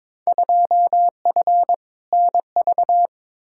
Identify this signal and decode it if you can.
Morse code, heard as a keyed tone, about 22 wpm, 700 Hz. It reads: 2F NV
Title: Field Day Exchanges - 22wpm